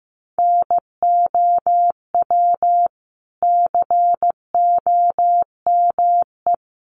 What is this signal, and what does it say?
Morse code, heard as a keyed tone, about 15 wpm, 700 Hz. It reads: NOW COME
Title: Sentences from Top 200 Words 15wpm